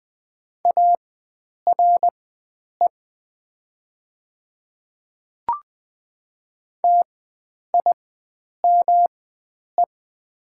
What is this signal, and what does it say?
Morse code, heard as a keyed tone, about 20 wpm, spaced as for 9 wpm, 700 Hz. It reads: ARE TIME
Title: 4x Character Spacing - Top 100 Words - 20wpm